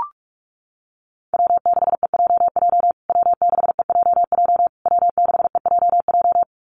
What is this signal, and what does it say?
Morse code, heard as a keyed tone, about 44 wpm, 700 Hz. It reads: W6EJJ W6EJJ W6EJJ
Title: USA Call Signs Repeated 3x 45wpm